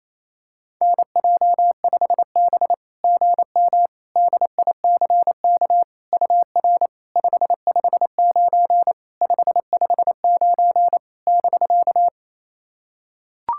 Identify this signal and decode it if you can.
Morse code, heard as a keyed tone, about 28 wpm, 690 Hz. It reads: NJ5B GM DICK UR 559 559 <BK>
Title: SOTA QSO Practice 28wpm